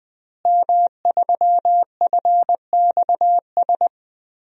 Morse code, heard as a keyed tone, about 20 wpm, 700 Hz. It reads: M3FXS
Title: DXCC-100 Call Signs - 5 Characters - Repeated 2x - 20wpm